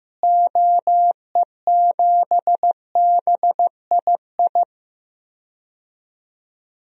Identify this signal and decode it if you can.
Morse code, heard as a keyed tone, about 15 wpm, 700 Hz. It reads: OE7BII
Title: DXCC-100 Call Signs - 6 Characters - Repeated 3x - 15wpm